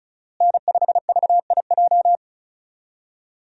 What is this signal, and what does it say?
Morse code, heard as a keyed tone, about 36 wpm, 700 Hz. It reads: N5VIJ
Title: USA Call Signs - 5 Characters - Repeated 2x - 35wpm